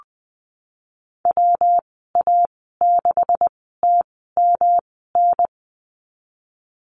Morse code, heard as a keyed tone, about 20 wpm, spaced as for 15 wpm, 700 Hz. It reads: WA6TMN
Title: 128 Lesson 21g - Number 6 - Call Signs 20wpm